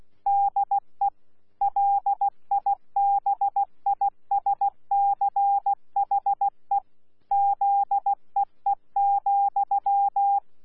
Morse code, heard as a keyed tone, about 16 wpm, 800 Hz. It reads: DE LIBISCHE ZEE,
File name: Z PI4VRZ / PI4VRZ 14-jun-2025